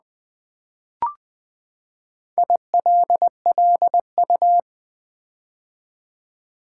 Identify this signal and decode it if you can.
Morse code, heard as a keyed tone, about 20 wpm, 700 Hz. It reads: ILLU
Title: Build A Word - 4 Letters At A Time - 10-Letter Words - 20wpm